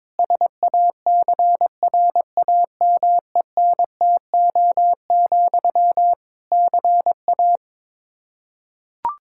Morse code, heard as a keyed tone, about 22 wpm, 700 Hz. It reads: SACRAMENTO, CA
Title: Practice QSO - Exchange 1 - Beginning - 22wpm